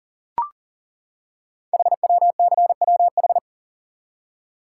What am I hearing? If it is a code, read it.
Morse code, heard as a keyed tone, about 38 wpm, 690 Hz. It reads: HWCWH
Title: Code Groups - 5 Letters - 40wpm